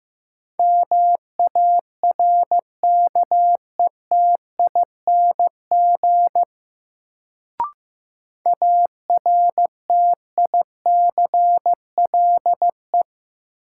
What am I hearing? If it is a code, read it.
Morse code, heard as a keyed tone, about 15 wpm, 700 Hz. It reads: MARKETING ARTICLE
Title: Sentences from Top 1000 Words - Review Part 5 - New Word Drill - 15wpm